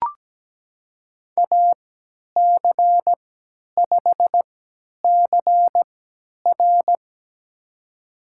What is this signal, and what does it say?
Morse code, heard as a keyed tone, about 17 wpm, spaced as for 10 wpm, 700 Hz. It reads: AC5CR
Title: USA Call Signs - 3x Character Spacing - 17wpm